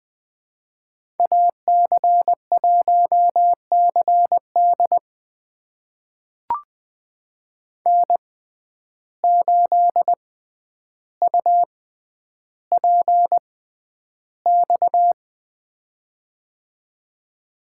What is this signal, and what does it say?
Morse code, heard as a keyed tone, about 20 wpm, spaced as for 10 wpm, 700 Hz. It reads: AC1CD N8UPX
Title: USA Call Signs - 5 Characters - 6x Character Spacing - 20wpm